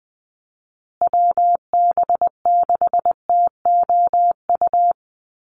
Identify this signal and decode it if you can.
Morse code, heard as a keyed tone, about 20 wpm, 700 Hz. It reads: WB6TOU
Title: Call Signs Repeated 2x - 20wpm